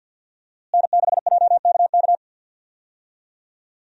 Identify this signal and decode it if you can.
Morse code, heard as a keyed tone, about 51 wpm, 700 Hz. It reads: N6JKK
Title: USA Call Signs - 5 Characters - Repeated 3x - 50wpm